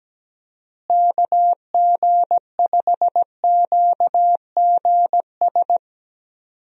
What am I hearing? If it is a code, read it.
Morse code, heard as a keyed tone, about 17 wpm, 700 Hz. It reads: KG5QGS
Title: USA Call Signs - 6 Characters - Repeated 3x - 17wpm